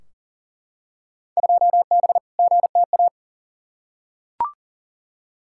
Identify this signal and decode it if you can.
Morse code, heard as a keyed tone, about 40 wpm, 700 Hz. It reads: 2B GTA
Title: Field Day Exchanges Encore - 40wpm